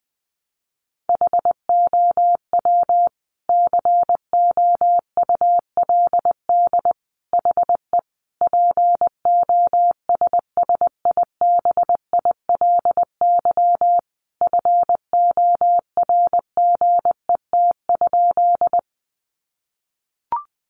Morse code, heard as a keyed tone, about 20 wpm, 700 Hz. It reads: HOW COULD HE POSSIBILY FORGET?
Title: Sentences from Top 1300 Words 20wpm